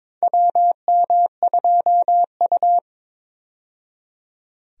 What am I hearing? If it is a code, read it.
Morse code, heard as a keyed tone, about 21 wpm, 700 Hz. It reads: WM2U
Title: USA Call Signs - 4 Characters - 22wpm